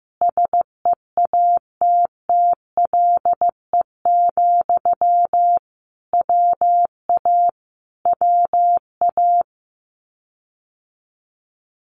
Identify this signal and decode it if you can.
Morse code, heard as a keyed tone, about 15 wpm, 700 Hz. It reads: SEATTLE, WA WA
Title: Common QSO Phrases Encore 15wpm